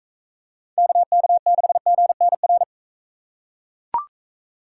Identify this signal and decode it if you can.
Morse code, heard as a keyed tone, about 40 wpm, 700 Hz. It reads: KK6GNR